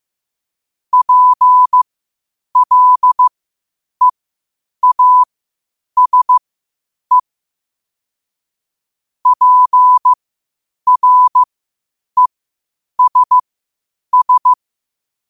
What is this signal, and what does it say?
Morse code, heard as a keyed tone, about 15 wpm, spaced as for 8 wpm, 1000 Hz. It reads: PLEASE PRESS
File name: qotdfiles / 8wpm